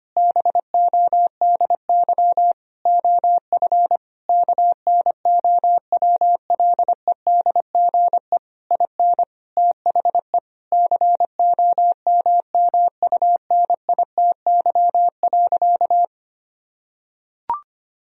Morse code, heard as a keyed tone, about 25 wpm, 700 Hz. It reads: BODY OF KNOWLEDGE IN THE COMMUNITY.
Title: Sentences from Top 1000 Words - Review Part 4 - 25wpm